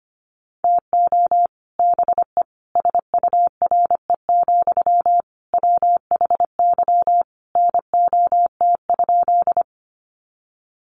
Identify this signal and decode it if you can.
Morse code, heard as a keyed tone, about 25 wpm, 700 Hz. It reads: TO BE SURE, WHY NOT?
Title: Sentences from Top 600 Words 25wpm